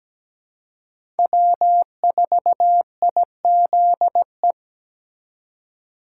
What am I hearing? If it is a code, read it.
Morse code, heard as a keyed tone, about 17 wpm, 700 Hz. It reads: W4IZE
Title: USA Call Signs - 5 Characters - 3x Character Spacing - 17wpm